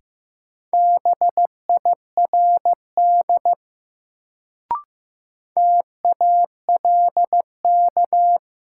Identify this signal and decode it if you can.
Morse code, heard as a keyed tone, about 15 wpm, 700 Hz. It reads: BIRD TALK
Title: Sentences from Top 400 Words - New Word Drill - 15wpm